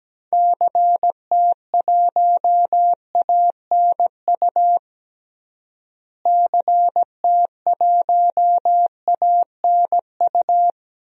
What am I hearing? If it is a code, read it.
Morse code, heard as a keyed tone, about 17 wpm, 700 Hz. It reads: CT1ANU CT1ANU
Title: DXCC-100 Call Signs - 6 Characters - Repeated 3x - 17wpm